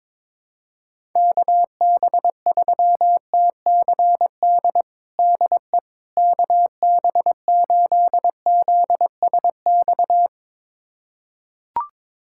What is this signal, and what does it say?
Morse code, heard as a keyed tone, about 22 wpm, 700 Hz. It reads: KB3TCD DE KB8ZSX
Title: USA Callsign DE Callsign - 6 Characters - 22wpm